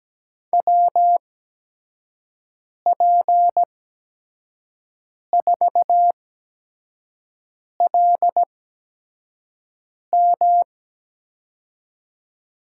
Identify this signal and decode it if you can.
Morse code, heard as a keyed tone, about 17 wpm, spaced as for 5 wpm, 700 Hz. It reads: WP4LM
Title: USA Call Signs - 5 Characters - 8x Character Spacing - 17wpm